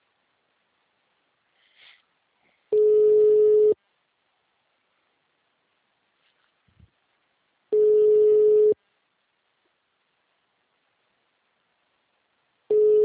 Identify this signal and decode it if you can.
Ringback tone (Europe, 425 Hz).